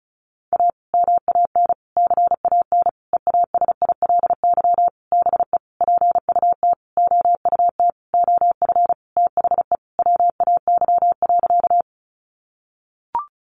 Morse code, heard as a keyed tone, about 34 wpm, 700 Hz. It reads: A MAN CAN EASILY BE PUT OUT OF THE WAY.
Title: Sentences from Top 900 Words 35wpm